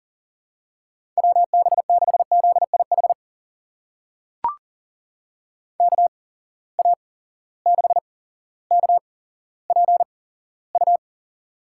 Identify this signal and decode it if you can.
Morse code, heard as a keyed tone, about 40 wpm, spaced as for 17 wpm, 700 Hz. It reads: WB6ZIH KA6KPU
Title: USA Call Signs - 6 Characters - 8x Character Spacing - 40wpm